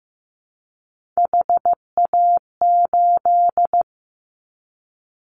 Morse code, heard as a keyed tone, about 15 wpm, 700 Hz. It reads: HA8
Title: Callsign Prefixes for Top 50 DXCC - 15wpm